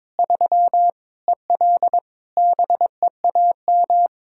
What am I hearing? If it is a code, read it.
Morse code, heard as a keyed tone, about 22 wpm, 700 Hz. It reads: 3 EL BEAM
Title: K1USN SST QSO Practice 22wpm